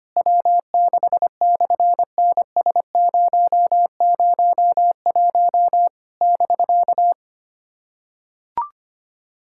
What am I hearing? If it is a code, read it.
Morse code, heard as a keyed tone, about 24 wpm, 700 Hz. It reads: W6/NS001 <BK>
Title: SOTA QSO Practice 25wpm